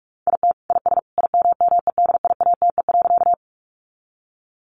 Morse code, heard as a keyed tone, about 44 wpm, 700 Hz. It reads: IT IS IMMEDIATE.
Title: Sentences from Top 1200 Words 45wpm